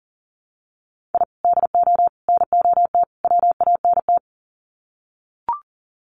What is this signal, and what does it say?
Morse code, heard as a keyed tone, about 40 wpm, 700 Hz. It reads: I DO NOT WANT
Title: Sets of 4 Words 40wpm